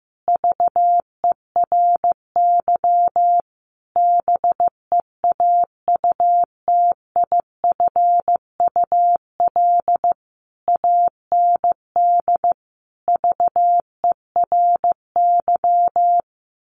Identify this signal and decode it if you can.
Morse code, heard as a keyed tone, about 15 wpm, 700 Hz. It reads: VERY BEAUTIFUL AND VERY
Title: A Sound Of Thunder 15wpm